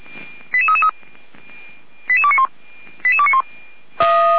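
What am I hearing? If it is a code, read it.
Five-tone selective call (ZVEI), heard as quick runs of five tones, tones of about 70 ms. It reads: 88333 88291 88291
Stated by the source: Five-tone ZVEI selective call received by SDR, from the fire brigade Feuerwehr Orlamünde.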